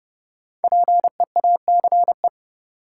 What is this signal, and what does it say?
Morse code, heard as a keyed tone, about 32 wpm, 700 Hz. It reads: PEACE
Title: Sentences from Top 1500 Words - New Word Drill Alt - 30wpm